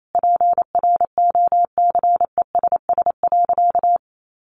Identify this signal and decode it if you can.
Morse code, heard as a keyed tone, about 28 wpm, 700 Hz. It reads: PROCESS.
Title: Sentences from Top 1400 Words 28wpm